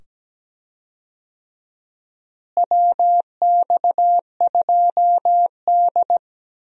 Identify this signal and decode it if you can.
Morse code, heard as a keyed tone, about 17 wpm, 700 Hz. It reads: WX2D
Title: USA Call Signs - 4 Characters - Repeated 2x - 17wpm